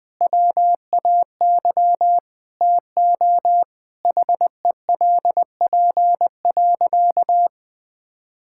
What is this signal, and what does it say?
Morse code, heard as a keyed tone, about 20 wpm, 700 Hz. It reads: WAY TO HELP.
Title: Sentences with 6-Word Phrases - 20wpm